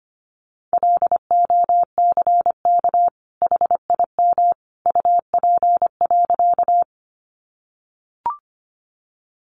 Morse code, heard as a keyed tone, about 25 wpm, 700 Hz. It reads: LOCK HIM UP.
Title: Sentences from Top 1300 Words 25wpm